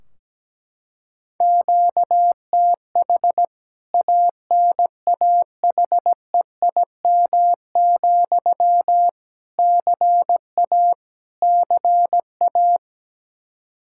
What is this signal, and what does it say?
Morse code, heard as a keyed tone, about 17 wpm, 700 Hz. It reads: QTH ANAHEIM, CA CA
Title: K1USN SST QSO Practice 17wpm